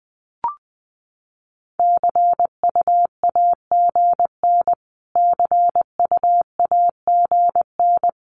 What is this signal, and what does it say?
Morse code, heard as a keyed tone, about 20 wpm, 700 Hz. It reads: CUAGN CUAGN
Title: Abbreviations Repeated 3x - 20wpm